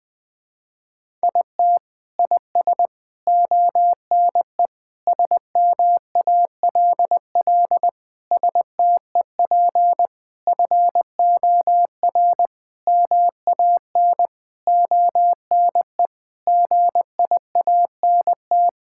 Morse code, heard as a keyed tone, about 21 wpm, 700 Hz. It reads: IT IS ONE SMALL STEP FOR MAN ONE GIANT
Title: Sentences with 9-Word Phrases - 20wpm